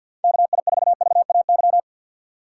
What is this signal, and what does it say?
Morse code, heard as a keyed tone, about 53 wpm, 700 Hz. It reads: KI4VAY
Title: USA Call Signs - 6 Characters - 8x Character Spacing - 50wpm